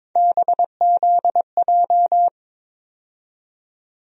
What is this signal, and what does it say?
Morse code, heard as a keyed tone, about 21 wpm, 700 Hz. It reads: BZJ
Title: USA Call Sign Postfixes - 3 Letters 22wpm